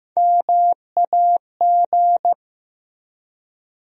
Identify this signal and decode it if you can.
Morse code, heard as a keyed tone, about 16 wpm, 700 Hz. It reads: MAG
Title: USA Call Sign Postfixes - 3 Letters 15wpm